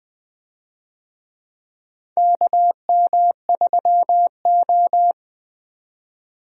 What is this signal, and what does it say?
Morse code, heard as a keyed tone, about 20 wpm, 700 Hz. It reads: KM3O
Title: USA Call Signs - 4 Characters - 20wpm